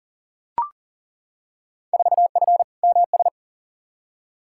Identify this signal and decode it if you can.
Morse code, heard as a keyed tone, about 40 wpm, 700 Hz. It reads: 4F MS
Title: Field Day Exchanges Encore - 40wpm